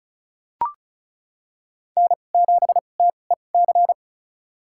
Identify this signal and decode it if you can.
Morse code, heard as a keyed tone, about 36 wpm, spaced as for 25 wpm, 700 Hz. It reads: N7TEC
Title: USA Call Signs - 5 Characters - 2x Character Spacing - 35wpm